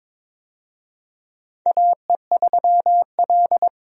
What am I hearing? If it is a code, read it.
Morse code, heard as a keyed tone, about 22 wpm, 700 Hz. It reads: AE3L
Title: USA Call Signs - 4 Characters - 22wpm